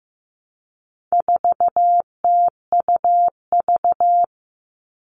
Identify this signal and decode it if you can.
Morse code, heard as a keyed tone, about 15 wpm, 700 Hz. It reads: 4TUV
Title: All Cut Numbers - 4-Digit 15wpm